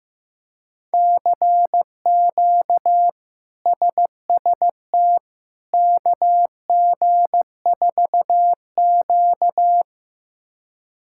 Morse code, heard as a keyed tone, about 15 wpm, 700 Hz. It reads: CQ SST KG4Q
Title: K1USN SST QSO Practice - 15wpm